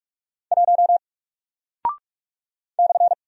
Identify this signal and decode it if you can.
Morse code, heard as a keyed tone, about 45 wpm, 700 Hz. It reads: 1 /
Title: Single Character (Letter:Number:Punctuation:Prosign) Alt 45wpm